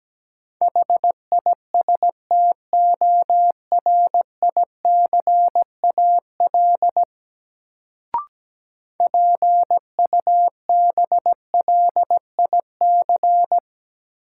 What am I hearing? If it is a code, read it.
Morse code, heard as a keyed tone, about 17 wpm, 700 Hz. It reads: HISTORICAL PUBLIC
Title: Sentences from Top 600 Words - New Word Drill - 17wpm